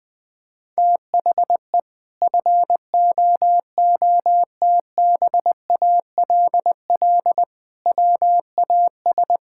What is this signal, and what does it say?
Morse code, heard as a keyed tone, about 20 wpm, 700 Hz. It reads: THE FOOTBALL WAS
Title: Sentences from Top 1600 Words 20wpm